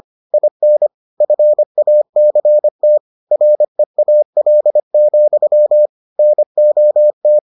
Morse code, heard as a keyed tone, about 25 wpm, 600 Hz. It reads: IN FACT REAL, NOT